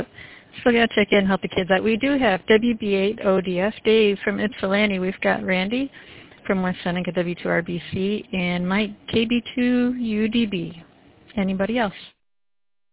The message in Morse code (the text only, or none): none